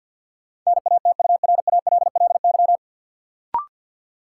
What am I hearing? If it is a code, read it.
Morse code, heard as a keyed tone, about 48 wpm, 700 Hz. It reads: NATURALLY
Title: Sentences from Top 1300 Words - New Word Drill - 50wpm